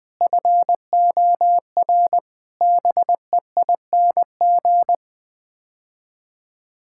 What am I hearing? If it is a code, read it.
Morse code, heard as a keyed tone, about 21 wpm, 700 Hz. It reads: FOR BEING
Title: Build A Phrase - 5-Word Phrases - 20wpm